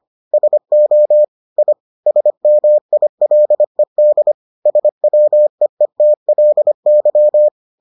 Morse code, heard as a keyed tone, about 25 wpm, 600 Hz. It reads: SO I SMILED SWEETLY